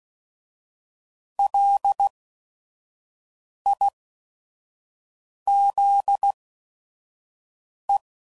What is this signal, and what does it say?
Morse code, heard as a keyed tone, about 16 wpm, spaced as for 5 wpm, 800 Hz. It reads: LIZE
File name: CW Code Groups @5WPM